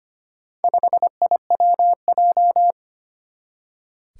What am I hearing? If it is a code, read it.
Morse code, heard as a keyed tone, about 25 wpm, 700 Hz. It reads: 5IWJ
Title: Four Challenging Letter-Number Combinations 25wpm